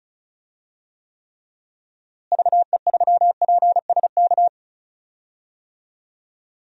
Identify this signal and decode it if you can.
Morse code, heard as a keyed tone, about 34 wpm, 700 Hz. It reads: VE3PSK
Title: DXCC-100 Call Signs - 6 Characters - Repeated 3x - 35wpm